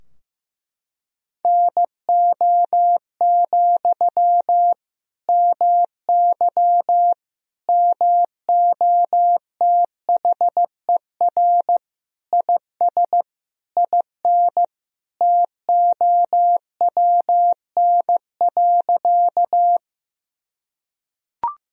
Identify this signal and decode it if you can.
Morse code, heard as a keyed tone, about 16 wpm, 700 Hz. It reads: NO, MY MOTHER IS IN TOWN.
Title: Sentences from Top 1000 Words - Review Part 2 - 15wpm